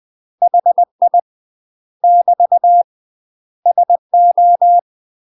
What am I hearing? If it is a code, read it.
Morse code, heard as a keyed tone, about 20 wpm, 700 Hz. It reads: HI <BT> SO
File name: UK MASTER QSOs TRAINING/20WPM